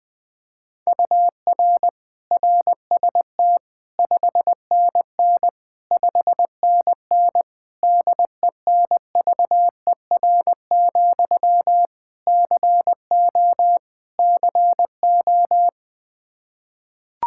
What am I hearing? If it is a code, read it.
Morse code, heard as a keyed tone, about 19 wpm, 700 Hz. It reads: UR RST 5NN 5NN DENVER, CO CO
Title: Practice QSO - Exchange 1 - Beginning - 20wpm